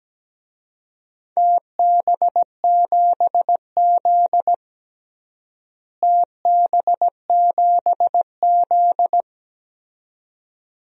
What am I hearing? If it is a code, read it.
Morse code, heard as a keyed tone, about 17 wpm, 700 Hz. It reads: TB7Z TB7Z
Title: DXCC-100 Call Signs - 4 Characters - Repeated 2x - 17wpm